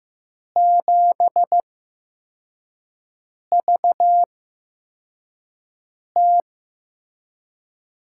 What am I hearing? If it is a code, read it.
Morse code, heard as a keyed tone, about 15 wpm, spaced as for 4 wpm, 700 Hz. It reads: 7VT
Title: USA Call Signs - 6 Characters - 8x Character Spacing - 15wpm